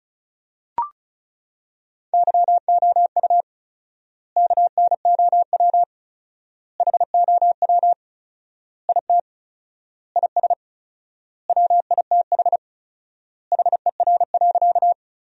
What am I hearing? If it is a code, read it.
Morse code, heard as a keyed tone, about 36 wpm, 700 Hz. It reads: YOU KNOW HOW IT IS WITH HER.
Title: 4x Word-Spacing - Sentences from Top 100 Words - 35wpm